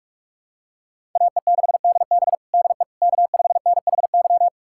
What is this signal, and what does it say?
Morse code, heard as a keyed tone, about 43 wpm, 700 Hz. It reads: AE6DB DE K5NHY
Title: USA Callsign DE Callsign - 5 Characters - 45wpm